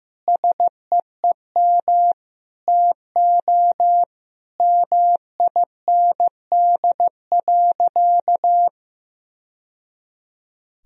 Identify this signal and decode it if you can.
Morse code, heard as a keyed tone, about 15 wpm, 700 Hz. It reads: SEEM TO MIND.
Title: Sentences from Top 1000 Words - Review Part 1 - 15wpm